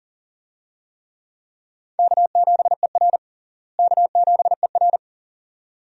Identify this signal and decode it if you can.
Morse code, heard as a keyed tone, about 40 wpm, 700 Hz. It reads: K7ER K7ER
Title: USA Call Signs - 4 Characters - Repeated 2x - 40wpm